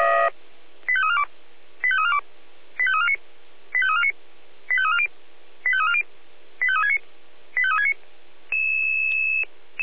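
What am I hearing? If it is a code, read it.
Five-tone selective call (ZVEI), heard as quick runs of five tones, tones of about 70 ms. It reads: 85342 85342 85349 85349 85340 85340 85369 85369
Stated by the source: ZVEI five-tone selective call 85357 received by SDR.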